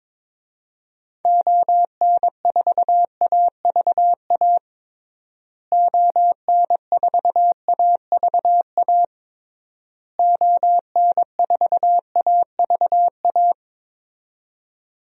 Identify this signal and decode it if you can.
Morse code, heard as a keyed tone, about 22 wpm, 700 Hz. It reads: ON4AVA ON4AVA ON4AVA
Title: DXCC-100 Call Signs - 6 Characters - Repeated 3x - 22wpm